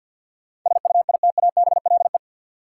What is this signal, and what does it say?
Morse code, heard as a keyed tone, about 53 wpm, 700 Hz. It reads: SUITABLE
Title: Sentences from Top 1400 Words - New Word Drill Alt - 50wpm